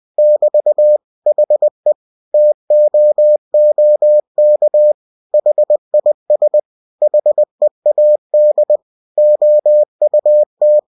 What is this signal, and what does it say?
Morse code, heard as a keyed tone, about 20 wpm, 600 Hz. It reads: <BT> HE TOOK HIS HEAD OUT
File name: cw-books / winnie-the-pooh / winnie-20wpm-page-0014